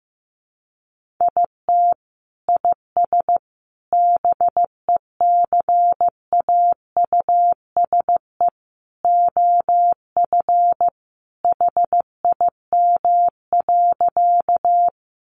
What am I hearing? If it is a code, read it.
Morse code, heard as a keyed tone, about 15 wpm, 700 Hz. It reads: IT IS BECAUSE OF HIM.
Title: Sentences from Top 600 Words 15wpm